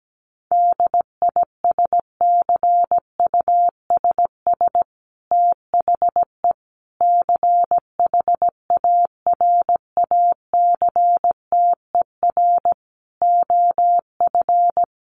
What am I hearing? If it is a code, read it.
Morse code, heard as a keyed tone, about 17 wpm, 700 Hz. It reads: DISCUSS THE CHARACTER OF
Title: Sentences from Top 1500 Words 17wpm